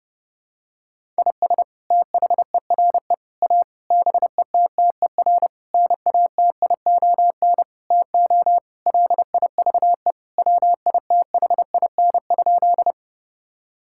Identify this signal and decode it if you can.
Morse code, heard as a keyed tone, about 30 wpm, 700 Hz. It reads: IS THERE A BETTER NATION TO LIVE WITHIN?
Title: Sentences from Top 1000 Words - Review Part 6 - 30wpm